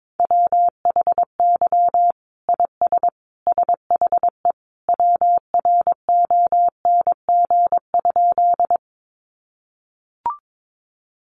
Morse code, heard as a keyed tone, about 22 wpm, 700 Hz. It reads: WHY IS SHE WRONG?
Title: Sentences from Top 900 Words 22wpm